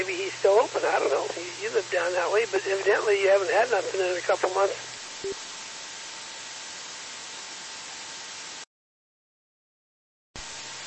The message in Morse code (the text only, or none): none